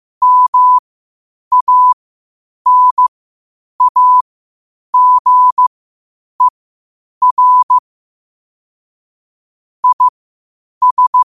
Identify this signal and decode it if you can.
Morse code, heard as a keyed tone, about 15 wpm, spaced as for 8 wpm, 1000 Hz. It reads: MANAGER IS
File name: qotdfiles / 8wpm